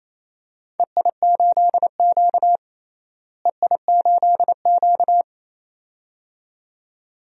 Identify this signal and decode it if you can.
Morse code, heard as a keyed tone, about 28 wpm, 700 Hz. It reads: EI8Q EI8Q
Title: DXCC-100 Call Signs - 4 Characters - Repeated 2x - 28wpm